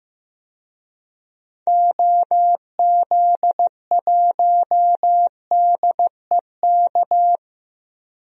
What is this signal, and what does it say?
Morse code, heard as a keyed tone, about 15 wpm, 700 Hz. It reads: OZ1DEK
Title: DXCC-100 Call Signs - 6 Characters - 15wpm